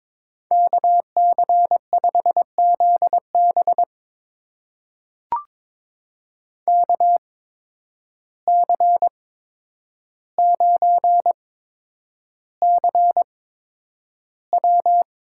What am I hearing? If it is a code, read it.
Morse code, heard as a keyed tone, about 22 wpm, spaced as for 9 wpm, 700 Hz. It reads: KC5ZB KC9CW
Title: USA Call Signs - 5 Characters - 8x Character Spacing - 22wpm